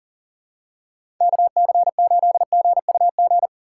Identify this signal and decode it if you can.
Morse code, heard as a keyed tone, about 40 wpm, 700 Hz. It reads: KC8GUG
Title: USA Call Signs - 6 Characters - 3x Character Spacing - 40wpm